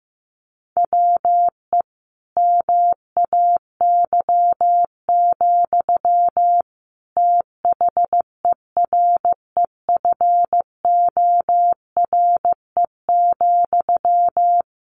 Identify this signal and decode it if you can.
Morse code, heard as a keyed tone, about 15 wpm, 700 Hz. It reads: WE MAY, THEREFORE,
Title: Sentences from Top 1500 Words 15wpm